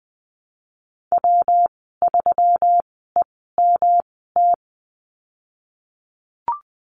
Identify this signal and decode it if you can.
Morse code, heard as a keyed tone, about 20 wpm, spaced as for 15 wpm, 700 Hz. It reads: W3EMT
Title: 122 Lesson 20g - Number 3 - Call Signs 20wpm